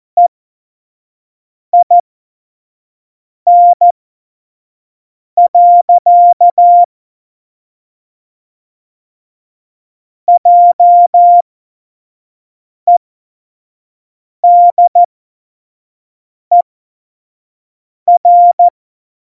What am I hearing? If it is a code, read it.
Morse code, heard as a keyed tone, about 14 wpm, spaced as for 5 wpm, 700 Hz. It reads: EIN. JEDER
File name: Morsekurs / Witze / Witz 03. November 2020, 25 F70